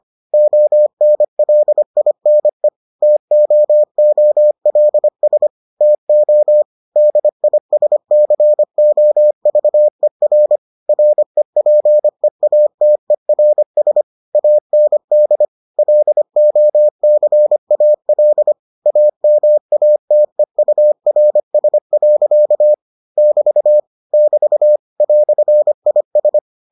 Morse code, heard as a keyed tone, about 25 wpm, 600 Hz. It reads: ONLINE TOOLS TO DISCOVER REPEATERS AND LOCAL AMATEURS. <BT> <BT> "IS